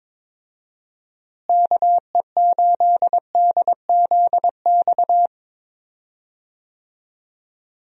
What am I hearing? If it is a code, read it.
Morse code, heard as a keyed tone, about 22 wpm, 700 Hz. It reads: KE8DZX
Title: USA Call Signs - 6 Characters - Repeated 3x - 22wpm